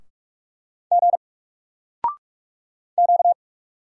Morse code, heard as a keyed tone, about 44 wpm, 700 Hz. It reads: G Q
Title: Single Character (Letter:Number:Punctuation:Prosign) Alt 45wpm